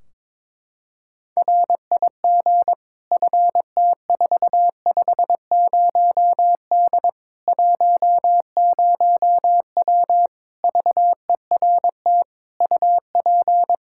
Morse code, heard as a keyed tone, about 22 wpm, 700 Hz. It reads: RIG FT450D 10W VERT UP
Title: Practice QSO - Exchange 2 - Info - 22wpm